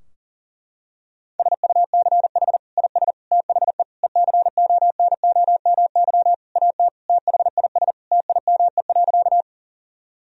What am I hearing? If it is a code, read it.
Morse code, heard as a keyed tone, about 40 wpm, 700 Hz. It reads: SUCH IS THE ECONOMY AT THIS TIME.
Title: Sentences from Top 1000 Words - Review Part 3 - 40wpm